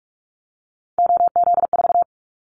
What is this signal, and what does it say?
Morse code, heard as a keyed tone, about 44 wpm, 700 Hz. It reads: OZ4
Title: Callsign Prefixes for Top 100 DXCC - 45wpm